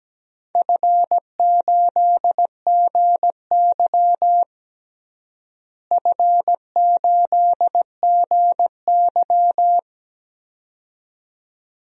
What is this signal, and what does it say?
Morse code, heard as a keyed tone, about 17 wpm, 700 Hz. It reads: F8GY F8GY
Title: DXCC-100 Call Signs - 4 Characters - Repeated 2x - 17wpm